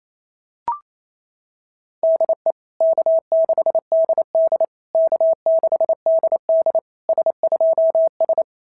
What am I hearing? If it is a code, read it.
Morse code, heard as a keyed tone, about 28 wpm, 690 Hz. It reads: DE K6DD K6DD S2S
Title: SOTA QSO Practice 28wpm